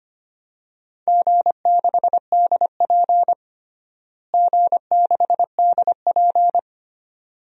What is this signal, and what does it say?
Morse code, heard as a keyed tone, about 25 wpm, 700 Hz. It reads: G6DP G6DP
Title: DXCC-100 Call Signs - 4 Characters - Repeated 2x - 25wpm